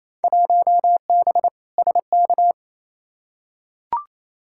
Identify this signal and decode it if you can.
Morse code, heard as a keyed tone, about 28 wpm, 700 Hz. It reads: 1B SK
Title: Field Day Exchanges Encore - 28wpm